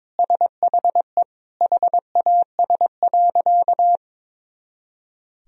Morse code, heard as a keyed tone, about 22 wpm, 700 Hz. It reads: SHE HAS.
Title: Sentences from Top 500 Words and Abbreviations 22wpm